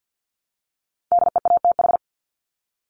Morse code, heard as a keyed tone, about 50 wpm, 700 Hz. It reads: DEATH
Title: Top 1001-1250 Words 50wpm